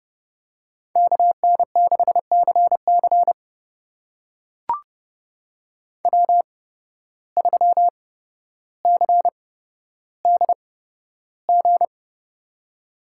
Morse code, heard as a keyed tone, about 30 wpm, spaced as for 13 wpm, 700 Hz. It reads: KN6CC W3CDG
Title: USA Call Signs - 5 Characters - 8x Character Spacing - 30wpm